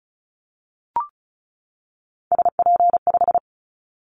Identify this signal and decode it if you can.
Morse code, heard as a keyed tone, about 36 wpm, 700 Hz. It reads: SP5
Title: Callsign Prefixes for Top 25 DXCC - 35wpm